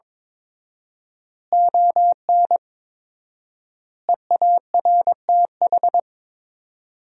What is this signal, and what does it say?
Morse code, heard as a keyed tone, about 22 wpm, 700 Hz. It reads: ON EARTH
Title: 4x Word-Spacing - Sentences from Top 200 Words - 22wpm